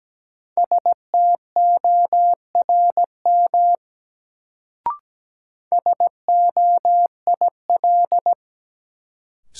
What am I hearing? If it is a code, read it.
Morse code, heard as a keyed tone, about 17 wpm, 700 Hz. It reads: STORM SOIL
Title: Sentences from Top 1500 Words - New Word Drill Alt - 17wpm